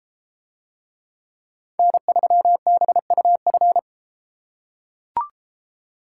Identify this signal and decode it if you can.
Morse code, heard as a keyed tone, about 32 wpm, 700 Hz. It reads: N3BUF